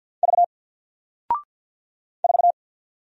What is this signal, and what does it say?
Morse code, heard as a keyed tone, about 50 wpm, 700 Hz. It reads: V 4
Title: Single Letter-Number 50wpm